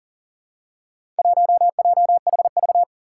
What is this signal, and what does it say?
Morse code, heard as a keyed tone, about 40 wpm, 700 Hz. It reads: 1JHV
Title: Four Challenging Letter-Number Combinations 40wpm